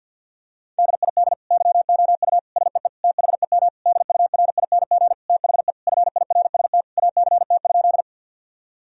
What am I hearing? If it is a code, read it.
Morse code, heard as a keyed tone, about 52 wpm, 700 Hz. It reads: DID YOU SEE THEM DURING THE FIRST ACT?
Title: Sentences from Top 1000 Words - Review Part 2 - 50wpm